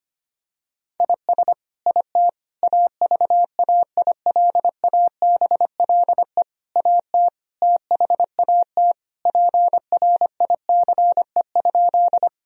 Morse code, heard as a keyed tone, about 25 wpm, 700 Hz. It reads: IS IT AVAILABLE AT THAT PRICE?
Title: Sentences from Top 1000 Words - Review Part 3 - 25wpm